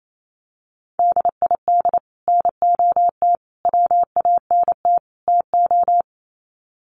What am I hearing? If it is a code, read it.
Morse code, heard as a keyed tone, about 28 wpm, 700 Hz. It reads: DID NOT WANT TO
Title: Sets of 4 Words 28wpm